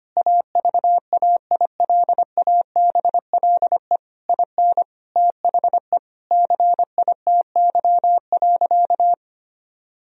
Morse code, heard as a keyed tone, about 25 wpm, 700 Hz. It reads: AVAILABLE IN THE CITY.
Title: Sentences from Top 1000 Words - Review Part 3 - 25wpm